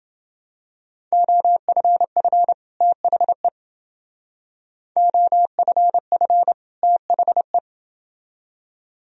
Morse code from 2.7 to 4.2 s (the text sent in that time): THE